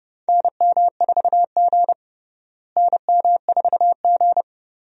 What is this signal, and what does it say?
Morse code, heard as a keyed tone, about 30 wpm, 700 Hz. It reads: NM4G NM4G
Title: USA Call Signs - 4 Characters - Repeated 2x - 30wpm